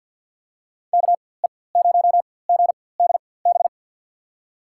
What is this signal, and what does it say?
Morse code, heard as a keyed tone, about 52 wpm, spaced as for 23 wpm, 700 Hz. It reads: KE0GDB
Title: USA Call Signs - 4x Character Spacing - 50wpm